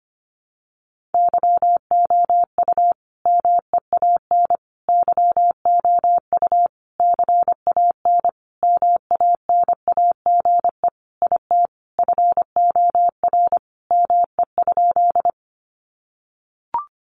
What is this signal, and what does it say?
Morse code, heard as a keyed tone, about 25 wpm, 700 Hz. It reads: YOU MEAN YOU CAN MANAGE IT FOR ME?
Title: Sentences from Top 1400 Words 25wpm